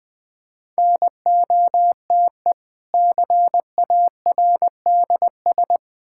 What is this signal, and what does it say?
Morse code, heard as a keyed tone, about 20 wpm, 700 Hz. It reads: NOTE CARDS
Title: Difficult Words with Hints - 20wpm